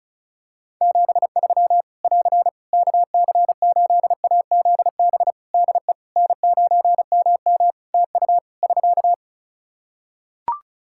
Morse code, heard as a keyed tone, about 34 wpm, 700 Hz. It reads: 73 <AR> KC8AZB DE N9MM TU <SK>
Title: K1USN SST QSO Practice 35wpm